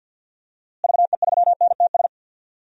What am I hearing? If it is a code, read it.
Morse code, heard as a keyed tone, about 52 wpm, 700 Hz. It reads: VE3NTS
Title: DXCC-100 Call Signs - 6 Characters - Repeated 3x - 50wpm